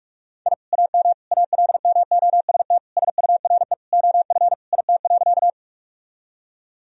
Morse code, heard as a keyed tone, about 46 wpm, 700 Hz. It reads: I AM ALMOST SURE OF IT.
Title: Sentences from Top 1000 Words - Review Part 4 - 45wpm